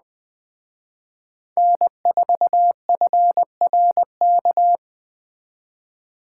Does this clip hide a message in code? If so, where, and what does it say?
Yes: Morse code from 1.6 to 4.8 s, heard as a keyed tone, about 20 wpm, 700 Hz. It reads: N4FRK